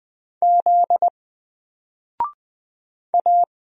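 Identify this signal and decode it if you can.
Morse code, heard as a keyed tone, about 20 wpm, 700 Hz. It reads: Z A
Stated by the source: Single letter and number Morse drill at 20 wpm, from N4LCD.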